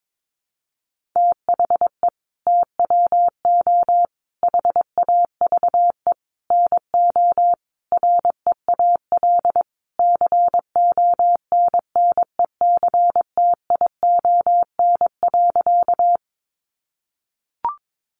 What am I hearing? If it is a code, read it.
Morse code, heard as a keyed tone, about 22 wpm, 700 Hz. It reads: THE TWO HAVE NO REAL CONNECTION.
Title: Sentences from Top 1300 Words 22wpm